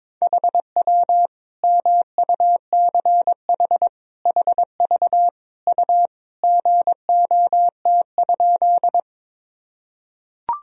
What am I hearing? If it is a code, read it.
Morse code, heard as a keyed tone, about 22 wpm, 700 Hz. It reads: HW MUCH HV U GOT?
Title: Sentences from Top 300 Words and Abbreviations 22wpm